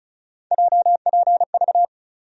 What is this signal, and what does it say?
Morse code, heard as a keyed tone, about 36 wpm, 700 Hz. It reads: JPV
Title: USA Call Sign Postfixes - 3 Letters 35wpm